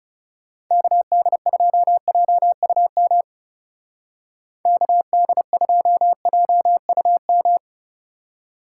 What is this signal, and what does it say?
Morse code, heard as a keyed tone, about 38 wpm, 700 Hz. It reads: KD2JUM KD2JUM